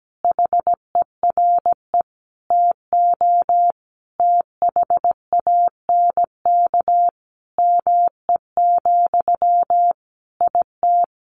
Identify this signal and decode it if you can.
Morse code, heard as a keyed tone, about 17 wpm, 700 Hz. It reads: HERE TO THANK ME, IT
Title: Sentences from Top 800 Words 17wpm